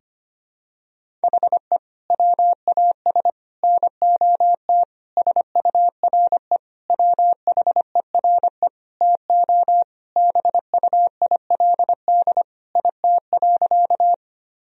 Morse code, heard as a keyed tone, about 25 wpm, 700 Hz. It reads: HE WAS NOT SURE WHERE TO BUILD IT.
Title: Sentences from Top 1000 Words - Review Part 1 - 25wpm